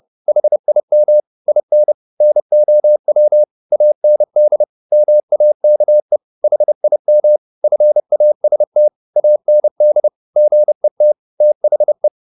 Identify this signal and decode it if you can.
Morse code, heard as a keyed tone, about 30 wpm, 600 Hz. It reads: HIM IN NOW AND MAKE HIM FAST AND GET THE